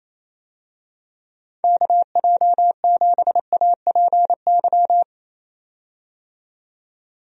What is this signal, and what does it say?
Morse code, heard as a keyed tone, about 28 wpm, 700 Hz. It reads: KJ7APY
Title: USA Call Signs - 6 Characters - 28wpm